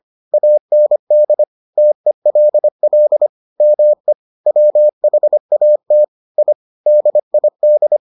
Morse code, heard as a keyed tone, about 25 wpm, 600 Hz. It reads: AND TELL ME WHAT I DID